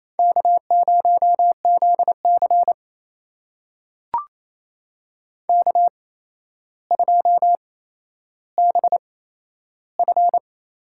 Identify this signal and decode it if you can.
Morse code, heard as a keyed tone, about 28 wpm, spaced as for 12 wpm, 700 Hz. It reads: K0ZC K2BF
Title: USA Call Signs - 4 Characters - 8x Character Spacing - 28wpm